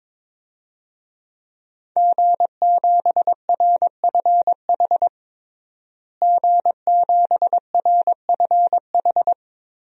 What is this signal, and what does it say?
Morse code, heard as a keyed tone, about 22 wpm, 700 Hz. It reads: G7RFH G7RFH
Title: DXCC-100 Call Signs - 5 Characters - Repeated 2x - 22wpm